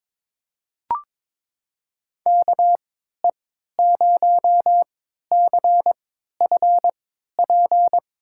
Morse code, heard as a keyed tone, about 21 wpm, spaced as for 12 wpm, 700 Hz. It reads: KE0CFP
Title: USA Call Signs - 6 Characters - 3x Character Spacing - 22wpm